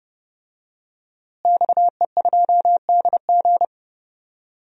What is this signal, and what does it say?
Morse code, heard as a keyed tone, about 30 wpm, 700 Hz. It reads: XE2DG
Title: DXCC-100 Call Signs - 5 Characters - Repeated 2x - 30wpm